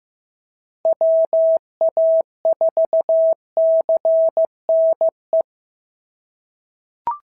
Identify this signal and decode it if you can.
Morse code, heard as a keyed tone, about 15 wpm, 650 Hz. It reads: WA4CNE
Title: SOTA QSO Practice 15wpm